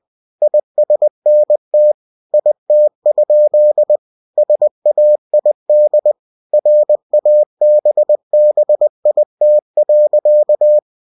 Morse code, heard as a keyed tone, about 20 wpm, 600 Hz. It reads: ISNT IT? SAID RABBIT.
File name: cw-books / winnie-the-pooh / winnie-20wpm-page-0060